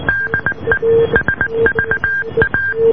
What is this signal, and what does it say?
Morse code, heard as a keyed tone, about 19 wpm, 1600 Hz, with noise. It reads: DE SVA